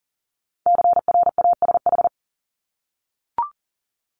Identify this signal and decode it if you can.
Morse code, heard as a keyed tone, about 39 wpm, 700 Hz. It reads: CRASH
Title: Multi-letter Phonemes 40wpm